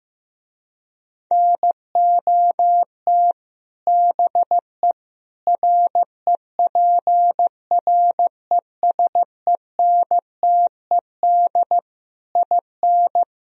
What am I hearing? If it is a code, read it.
Morse code, heard as a keyed tone, about 15 wpm, 700 Hz. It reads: NOT BE REPRESENTED IN
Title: Speed-Racing - Sets of 5 Words 15wpm